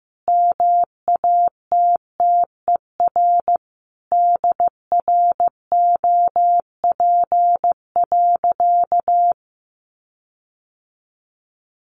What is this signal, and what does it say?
Morse code, heard as a keyed tone, about 15 wpm, 700 Hz. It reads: MATTER DROP.
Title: Sentences from Top 1400 Words 15wpm